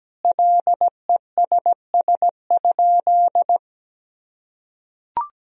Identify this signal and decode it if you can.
Morse code, heard as a keyed tone, about 17 wpm, 700 Hz. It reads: LESS?
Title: Sentences from Top 1000 Words - Review Part 2 - 17wpm